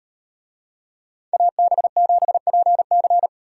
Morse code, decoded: AB7PC

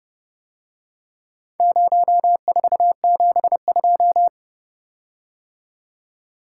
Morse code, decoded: 0472

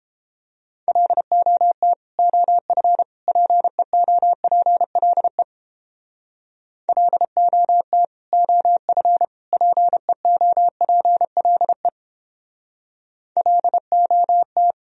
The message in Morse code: LOT OF PEOPLE LOT OF PEOPLE LOT

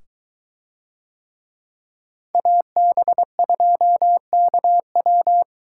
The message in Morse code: AB2KW

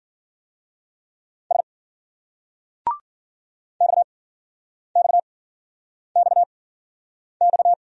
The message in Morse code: S XXXX

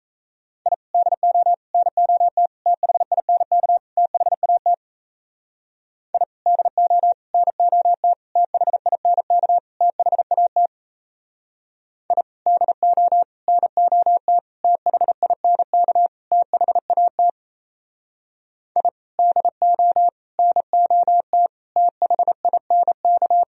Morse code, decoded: I DO NOT THINK THAT I DO NOT THINK THAT I DO NOT THINK THAT I DO NOT THINK

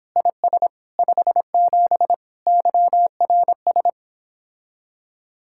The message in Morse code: IS 57 YRS

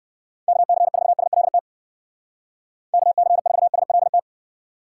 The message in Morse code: KC4SLT KC4SLT

